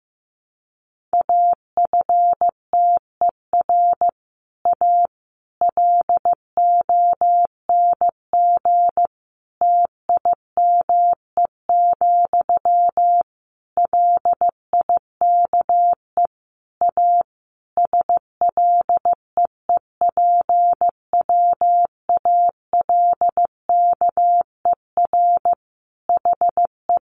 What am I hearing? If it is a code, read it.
Morse code, heard as a keyed tone, about 15 wpm, 700 Hz. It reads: AFTER A LONG TIME, LIKE A SLEEPWALKER HE
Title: A Sound Of Thunder 15wpm